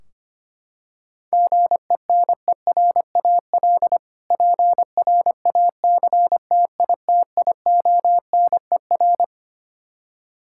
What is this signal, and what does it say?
Morse code, heard as a keyed tone, about 25 wpm, 700 Hz. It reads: GENERAL PRACTITIONER
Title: K1USN SST QSO Practice 25wpm